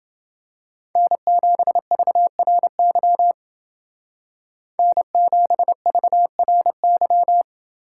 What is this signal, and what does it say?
Morse code, heard as a keyed tone, about 29 wpm, 700 Hz. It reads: N7VRY N7VRY